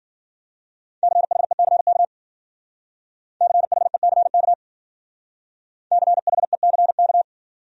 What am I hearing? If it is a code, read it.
Morse code, heard as a keyed tone, about 50 wpm, 700 Hz. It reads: CHECK CHECK CHECK
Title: Speed-Racing Top 500 Words 40wpm